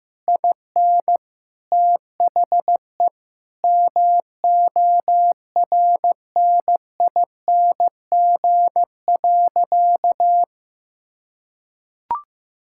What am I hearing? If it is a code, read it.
Morse code, heard as a keyed tone, about 16 wpm, 700 Hz. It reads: IN THE MORNING.
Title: Sentences from Top 1000 Words - Review Part 5 - 15wpm